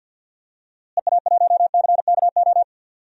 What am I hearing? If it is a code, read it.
Morse code, heard as a keyed tone, about 52 wpm, 700 Hz. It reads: EA1CKO
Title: DXCC-100 Call Signs - 6 Characters - Repeated 2x - 50wpm